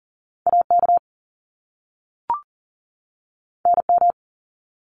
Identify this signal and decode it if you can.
Morse code, heard as a keyed tone, about 40 wpm, 700 Hz. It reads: AK NM
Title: ARRL Sections 40wpm